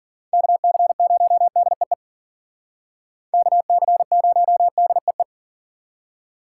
Morse code, heard as a keyed tone, about 52 wpm, 700 Hz. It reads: KC0DEE KC0DEE